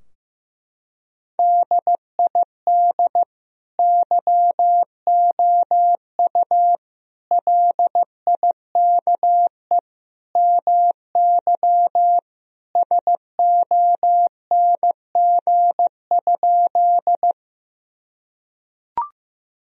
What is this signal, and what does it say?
Morse code, heard as a keyed tone, about 15 wpm, 700 Hz. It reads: DID YOU LIKE MY SONG?
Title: Sentences from Top 1000 Words - Review Part 2 - 15wpm